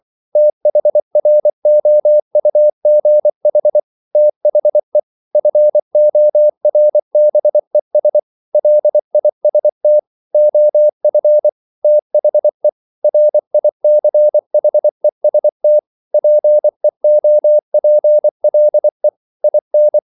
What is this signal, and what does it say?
Morse code, heard as a keyed tone, about 24 wpm, 600 Hz. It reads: THROUGH THE FORBES LIST OF THE RICHEST PEOPLE IN
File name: CW / Practice / EnglishQuotations0324